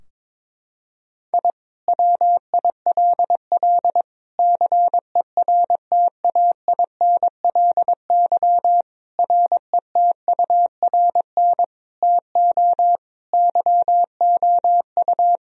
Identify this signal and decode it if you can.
Morse code, heard as a keyed tone, about 22 wpm, 700 Hz. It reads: I WILL CERTAINLY RETURN TO YOU
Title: Sentences from Top 1000 Words - Review Part 4 - 22wpm